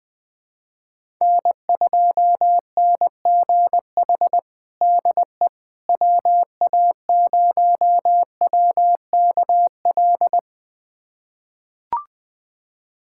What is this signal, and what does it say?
Morse code, heard as a keyed tone, about 20 wpm, 700 Hz. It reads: N2NGH DE WA0WKL
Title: USA Callsign DE Callsign 20wpm